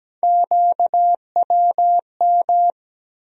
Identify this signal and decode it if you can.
Morse code, heard as a keyed tone, about 17 wpm, 700 Hz. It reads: QWM
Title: USA Call Sign Postfixes - 3 Letters 17wpm